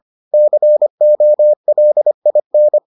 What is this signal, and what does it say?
Morse code, heard as a keyed tone, about 25 wpm, 600 Hz. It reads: COLIN